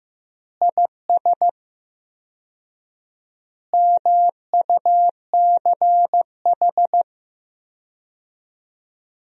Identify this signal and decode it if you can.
Morse code, heard as a keyed tone, about 15 wpm, 700 Hz. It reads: IS MUCH
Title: 4x Word-Spacing - Sentences from Top 400 Words - 15wpm